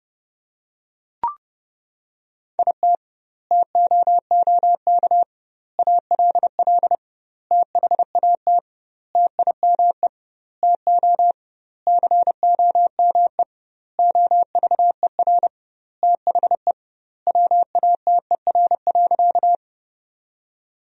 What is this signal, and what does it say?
Morse code, heard as a keyed tone, about 32 wpm, 700 Hz. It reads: IT TOOK ALL THAT TIME TO COME OVER THE WATER.
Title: 2x Word-Spacing - Sentences from Top 300 Words - 30wpm